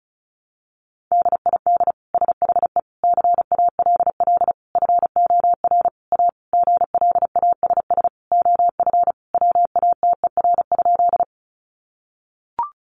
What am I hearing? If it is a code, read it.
Morse code, heard as a keyed tone, about 34 wpm, 700 Hz. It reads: DID SHE CALL FOR A GLASS OF WATER?
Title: Sentences from Top 1300 Words 35wpm